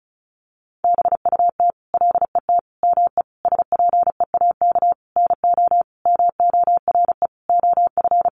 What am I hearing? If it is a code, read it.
Morse code, heard as a keyed tone, about 34 wpm, 700 Hz. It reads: BUT LET ME SPEAK NO MORE OF
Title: Sentences from Top 1100 Words 35wpm